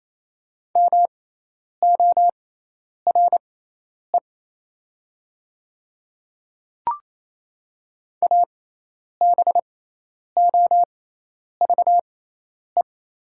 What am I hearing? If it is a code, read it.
Morse code, heard as a keyed tone, about 27 wpm, spaced as for 10 wpm, 700 Hz. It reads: MORE ABOVE
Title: 6x Character Spacing - Top 500 Words - 28wpm